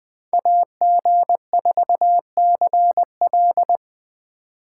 Morse code, heard as a keyed tone, about 21 wpm, 700 Hz. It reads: AG4CL
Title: Speed Racing - USA 2x2 Call Signs 17wpm